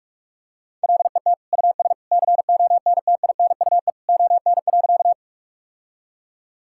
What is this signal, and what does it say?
Morse code, heard as a keyed tone, about 41 wpm, 700 Hz. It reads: LET US CONTINUE ON.
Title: Sentences from Top 1000 Words - Review Part 6 - 45wpm